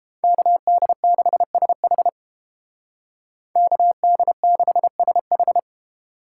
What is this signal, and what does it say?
Morse code, heard as a keyed tone, about 29 wpm, 700 Hz. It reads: KD6SH KD6SH